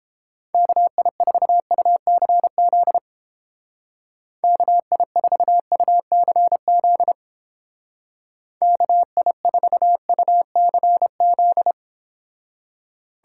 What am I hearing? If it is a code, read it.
Morse code, heard as a keyed tone, about 30 wpm, 700 Hz. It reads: KI4UCZ KI4UCZ KI4UCZ